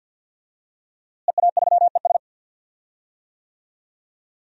Morse code, heard as a keyed tone, about 51 wpm, 700 Hz. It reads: EA3ES
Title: DXCC-100 Call Signs - 5 Characters - Repeated 2x - 50wpm